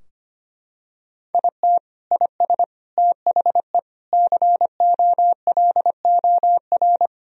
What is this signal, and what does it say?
Morse code, heard as a keyed tone, about 25 wpm, 700 Hz. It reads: IT IS THE COLOR
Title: Sentences from Top 1000 Words - Review Part 3 - 25wpm